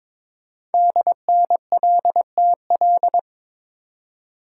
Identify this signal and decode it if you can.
Morse code, heard as a keyed tone, about 22 wpm, 700 Hz. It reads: DNLTL
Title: Code Groups - 5 Letters - 22wpm